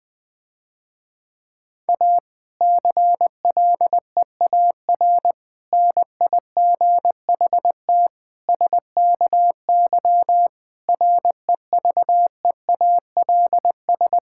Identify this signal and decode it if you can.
Morse code, heard as a keyed tone, about 21 wpm, 700 Hz. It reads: A CLEAR NIGHT SKY REVEALS
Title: Difficult Sentences with Hints - 20wpm